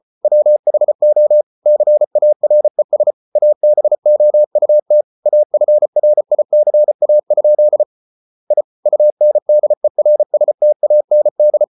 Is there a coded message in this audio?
Yes: Morse code from 0.2 to 11.6 s, heard as a keyed tone, about 34 wpm, 600 Hz. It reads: WHO CARES ABOUT AFRICA? I UNDERSTAND